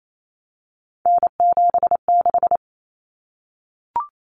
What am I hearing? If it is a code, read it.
Morse code, heard as a keyed tone, about 28 wpm, 700 Hz. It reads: N76
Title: Common Cut Numbers - 3-Digit 28wpm